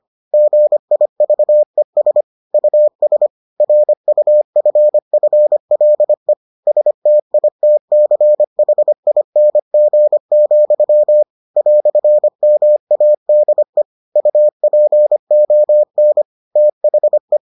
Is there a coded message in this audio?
Yes: Morse code from 0.3 to 17.4 s, heard as a keyed tone, about 25 wpm, 600 Hz. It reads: GIVES US RUFFLE STITCHING, "MADE UPON THE